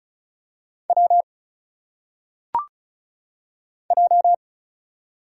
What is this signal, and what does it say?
Morse code, heard as a keyed tone, about 34 wpm, 700 Hz. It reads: W J
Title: Single Letters - Phonetic Pronunciation 35wpm